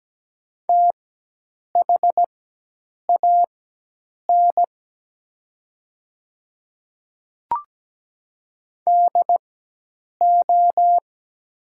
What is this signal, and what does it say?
Morse code, heard as a keyed tone, about 17 wpm, spaced as for 8 wpm, 700 Hz. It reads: THAN DO
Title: 4x Character Spacing - Top 100 Words - 17wpm